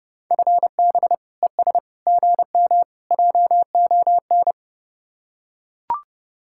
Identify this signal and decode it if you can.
Morse code, heard as a keyed tone, about 30 wpm, 700 Hz. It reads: FB ES GM JON